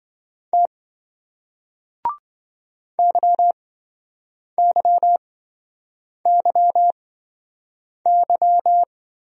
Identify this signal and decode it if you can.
Morse code, heard as a keyed tone, about 27 wpm, spaced as for 7 wpm, 700 Hz. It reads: T YYYY